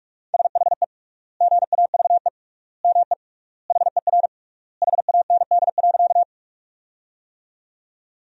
Morse code, heard as a keyed tone, about 41 wpm, 700 Hz. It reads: SHE GAVE ME HER HAND.
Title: 3x Word-Spacing - Sentences from Top 500 Words - 45wpm